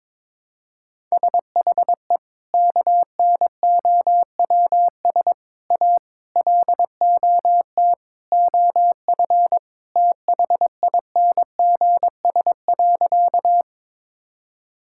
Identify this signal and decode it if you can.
Morse code, heard as a keyed tone, about 22 wpm, 700 Hz. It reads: SHE KNOWS A LOT OF THINGS.
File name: practice / 22wpm / Sentences with 4-Word Phrases - 22wpm